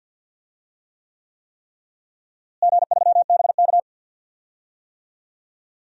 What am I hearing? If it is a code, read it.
Morse code, heard as a keyed tone, about 47 wpm, 700 Hz. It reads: G3BK
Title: DXCC-100 Call Signs - 4 Characters - Repeated 3x - 50wpm